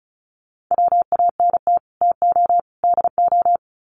Morse code, heard as a keyed tone, about 34 wpm, 700 Hz. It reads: WANT TO DO
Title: Sets of 3 Words 35wpm